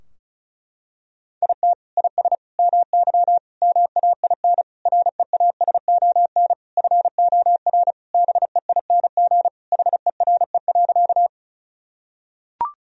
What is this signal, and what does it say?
Morse code, heard as a keyed tone, about 36 wpm, 700 Hz. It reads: IT IS MY MAIN REASON FOR BEING HERE.
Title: Sentences from Top 1000 Words - Review Part 2 - 35wpm